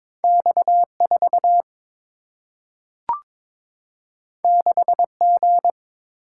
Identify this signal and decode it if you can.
Morse code, heard as a keyed tone, about 22 wpm, 700 Hz. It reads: X 4 6 G
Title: Single-Letter Single-Number 22wpm